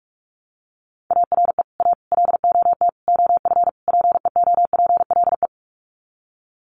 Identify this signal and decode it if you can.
Morse code, heard as a keyed tone, about 44 wpm, 700 Hz. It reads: ARE A LOT OF PEOPLE
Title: Sets of 5 Words 45wpm